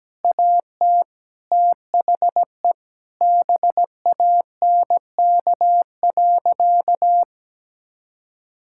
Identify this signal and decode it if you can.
Morse code, heard as a keyed tone, about 17 wpm, 700 Hz. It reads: AT THE BANK.
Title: Sentences from Top 1000 Words - Review Part 6 - 17wpm